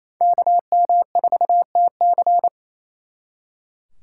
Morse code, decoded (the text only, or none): KM4TC